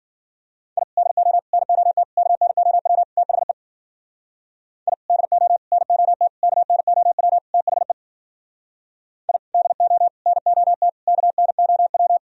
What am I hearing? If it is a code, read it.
Morse code, heard as a keyed tone, about 52 wpm, 700 Hz. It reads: I DO NOT KNOW THE I DO NOT KNOW THE I DO NOT KNOW